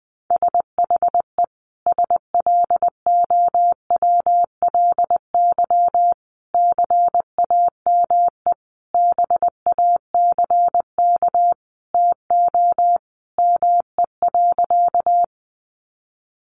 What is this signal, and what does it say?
Morse code, heard as a keyed tone, about 20 wpm, 700 Hz. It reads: SHE SLOWLY CAME BACK TO ME.
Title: Sentences from Top 1100 Words 20wpm